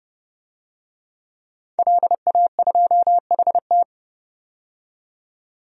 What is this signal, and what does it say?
Morse code, heard as a keyed tone, about 32 wpm, 700 Hz. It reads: LA2HT
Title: DXCC-100 Call Signs - 5 Characters - 30wpm